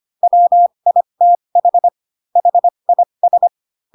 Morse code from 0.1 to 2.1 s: WITH